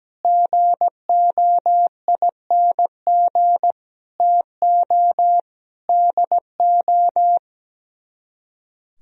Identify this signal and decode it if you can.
Morse code, heard as a keyed tone, about 17 wpm, 700 Hz. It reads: GOING TO DO
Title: Speed-Racing - Sets of 5 Words 17wpm